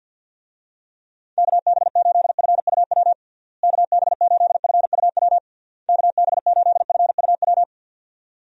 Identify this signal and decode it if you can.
Morse code, heard as a keyed tone, about 45 wpm, 700 Hz. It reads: KB8FUW KB8FUW KB8FUW
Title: USA Call Signs - 6 Characters - Repeated 3x - 50wpm